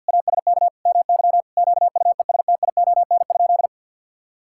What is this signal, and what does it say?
Morse code, heard as a keyed tone, about 50 wpm, 700 Hz. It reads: ASK MY QUESTION?
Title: Sentences from Top 1000 Words - Review Part 5 - 50wpm